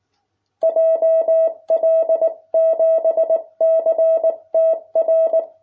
Morse code, heard as a keyed tone, about 19 wpm, 660 Hz. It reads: JL7CTR